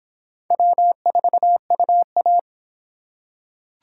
Morse code, heard as a keyed tone, about 21 wpm, 700 Hz. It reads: W4UA
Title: Speed Racing - USA 1x2 Call Signs 17wpm